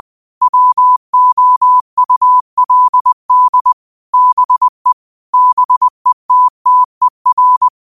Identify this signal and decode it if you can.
Morse code, heard as a keyed tone, about 20 wpm, 1000 Hz. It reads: WOULD BE BETTER
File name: qotdfiles / 20wpm